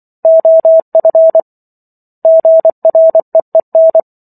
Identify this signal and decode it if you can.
Morse code, heard as a keyed tone, about 24 wpm, 650 Hz. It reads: OF GREEN